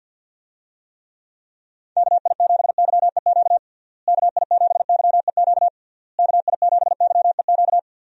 Morse code, heard as a keyed tone, about 52 wpm, 700 Hz. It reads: KI7YEQ KI7YEQ KI7YEQ
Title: USA Call Signs - 6 Characters - Repeated 3x - 50wpm